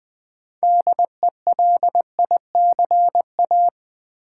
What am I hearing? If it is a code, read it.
Morse code, heard as a keyed tone, about 20 wpm, 700 Hz. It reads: DELICA
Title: Build A Word - 2 Letters At A Time - 8-Letter Words - 20wpm